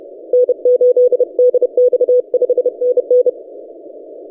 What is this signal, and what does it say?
Morse code, heard as a keyed tone, about 30 wpm, 500 Hz. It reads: N8DX 5NN
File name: posts / 4w-n1yc-17cw- -1516z